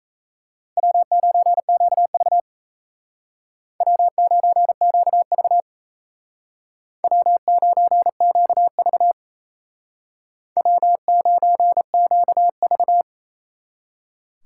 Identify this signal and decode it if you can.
Morse code, heard as a keyed tone, about 35 wpm, 700 Hz. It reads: W9QV W9QV W9QV W9QV
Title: Speed Racing - USA 1x2 Call Signs 28wpm